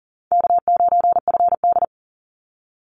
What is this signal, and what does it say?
Morse code, heard as a keyed tone, about 40 wpm, 700 Hz. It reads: K9FD
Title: Call Signs Repeated 2x - 40wpm